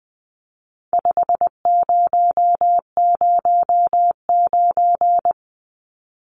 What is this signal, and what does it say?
Morse code, heard as a keyed tone, about 20 wpm, 700 Hz. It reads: 5009
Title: Numbers - 4-Digit 20wpm